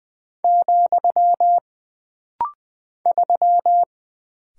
Morse code, heard as a keyed tone, about 20 wpm, 700 Hz. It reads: , 3
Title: Single Character (Letter:Number:Punctuation:Prosign) - Rapid-Fire - 20wpm